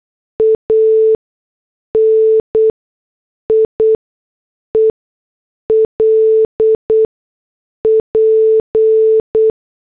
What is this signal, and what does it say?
Morse code, heard as a keyed tone, about 8 wpm, spaced as for 6 wpm, 440 Hz. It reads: ANIEL P